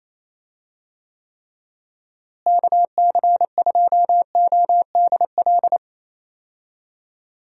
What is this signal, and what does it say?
Morse code, heard as a keyed tone, about 27 wpm, 700 Hz. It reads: KC2ODL
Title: USA Call Signs - 6 Characters - Repeated 2x - 28wpm